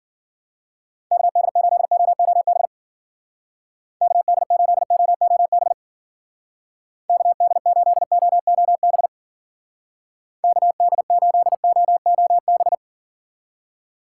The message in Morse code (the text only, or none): KD8OOB KD8OOB KD8OOB KD8OOB